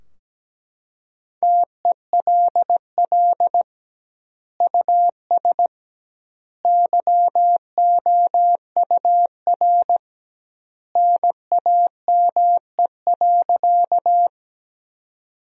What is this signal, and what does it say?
Morse code, heard as a keyed tone, about 17 wpm, 700 Hz. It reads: TELL US YOUR NAME.
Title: 2x Word-Spacing - Sentences from Top 200 Words - 17wpm